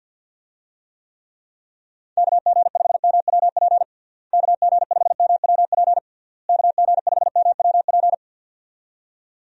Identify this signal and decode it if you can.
Morse code, heard as a keyed tone, about 52 wpm, 700 Hz. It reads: KG5MWP KG5MWP KG5MWP
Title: USA Call Signs - 6 Characters - Repeated 3x - 50wpm